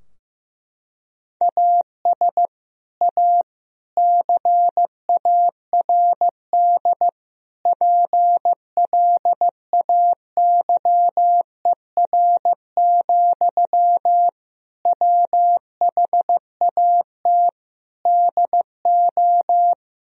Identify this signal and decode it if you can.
Morse code, heard as a keyed tone, about 15 wpm, 700 Hz. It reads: AS A CARD PLAYER, WHAT DO
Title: Sentences from Top 1000 Words - Review Part 4 - 15wpm